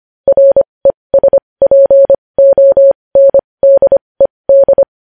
Morse code, heard as a keyed tone, about 25 wpm, spaced as for 20 wpm, 550 Hz. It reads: RESPONDED
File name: podcast / News headlines-1x-25x20